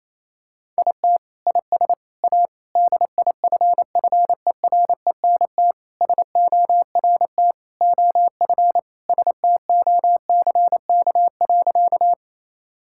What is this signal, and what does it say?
Morse code, heard as a keyed tone, about 28 wpm, 700 Hz. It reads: IT IS A DIFFERENT SORT OF STOCK.
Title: Sentences from Top 1000 Words - Review Part 5 - 28wpm